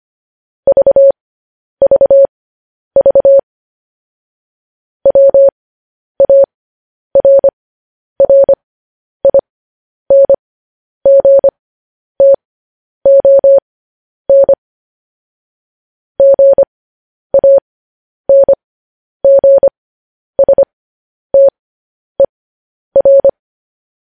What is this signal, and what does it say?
Morse code, heard as a keyed tone, about 25 wpm, spaced as for 10 wpm, 550 Hz. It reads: VVV WARRINGTON GANGSTER
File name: podcast / News headlines-1x-25x10